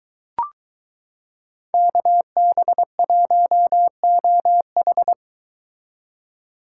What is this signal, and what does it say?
Morse code, heard as a keyed tone, about 23 wpm, 700 Hz. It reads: KB1OH